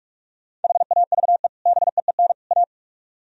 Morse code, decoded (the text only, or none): HAVE BEEN A